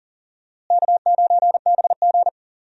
Morse code, heard as a keyed tone, about 40 wpm, 700 Hz. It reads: K9BG
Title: USA Call Signs - 4 Characters - Repeated 2x - 40wpm